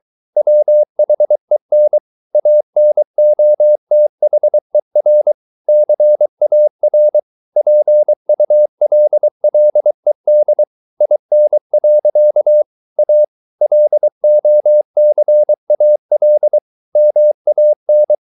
Morse code, heard as a keyed tone, about 23 wpm, 600 Hz. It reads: WHEN ANOTHER CAR PULLED IN. A LOCAL MAN
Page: Romanen Never af Ken Follet på engelsk.